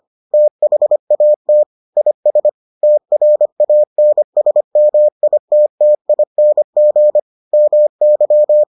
Morse code, heard as a keyed tone, about 25 wpm, 600 Hz. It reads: THAT IS TRANSMITTING MY